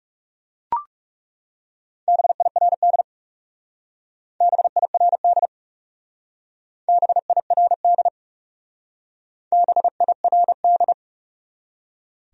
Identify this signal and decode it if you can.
Morse code, heard as a keyed tone, about 35 wpm, 700 Hz. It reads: BIRD BIRD BIRD BIRD
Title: Speed-Racing Top 500 Words 30wpm